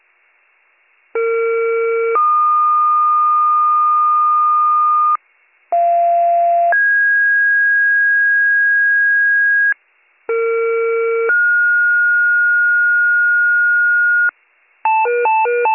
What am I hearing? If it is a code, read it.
Two-tone page, heard as a one-second tone, then a three-second tone, about 460 then 1220 Hz, about 690 then 1650 Hz, about 460 then 1430 Hz, with noise. It is heard from 1.1 to 14.3 s.